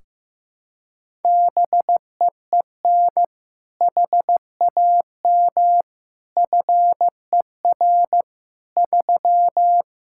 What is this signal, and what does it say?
Morse code, heard as a keyed tone, about 15 wpm, 700 Hz. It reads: BEEN HAM FER 3
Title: K1USN SST QSO Practice 15wpm